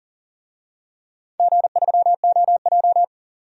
Morse code, heard as a keyed tone, about 40 wpm, 700 Hz. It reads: G3OJ
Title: DXCC-100 Call Signs - 4 Characters - 40wpm